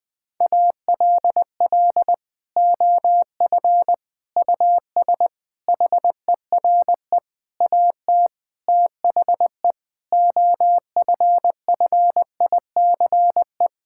Morse code, heard as a keyed tone, about 20 wpm, 700 Hz. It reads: ALL OF US HERE AT THE OFFICE
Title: Sentences with 5-Word Phrases - 20wpm